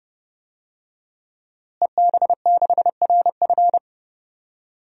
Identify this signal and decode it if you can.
Morse code, heard as a keyed tone, about 31 wpm, 700 Hz. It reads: EB6RF
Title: DXCC-100 Call Signs - 5 Characters - Repeated 3x - 30wpm